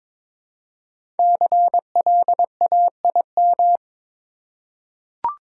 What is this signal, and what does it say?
Morse code, heard as a keyed tone, about 22 wpm, 700 Hz. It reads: CLAIM
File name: 5 Letter Words - First 3 Letters Given - 22wpm